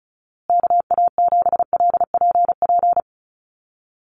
Morse code, heard as a keyed tone, about 34 wpm, 700 Hz. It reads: KA7LPP
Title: USA Call Signs Repeated 2x 35wpm